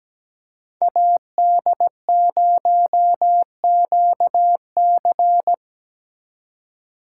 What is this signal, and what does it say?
Morse code, heard as a keyed tone, about 20 wpm, 700 Hz. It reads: AD0QC
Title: Speed Racing - USA 2x2 Call Signs 17wpm